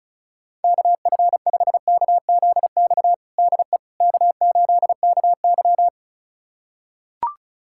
Morse code, heard as a keyed tone, about 34 wpm, 690 Hz. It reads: KF5KZX DE K8KY
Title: USA Callsign DE Callsign 35wpm